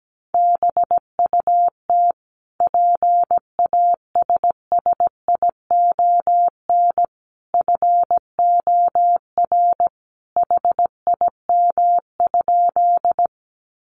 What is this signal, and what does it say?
Morse code, heard as a keyed tone, about 17 wpm, 700 Hz. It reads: BUT PASSION FOR HIM?
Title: Sentences from Top 1300 Words 17wpm